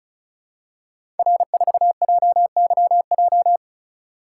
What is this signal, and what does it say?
Morse code, heard as a keyed tone, about 34 wpm, 700 Hz. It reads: R4JYJ
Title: Code Groups - 5 Letter:Numbers - 35wpm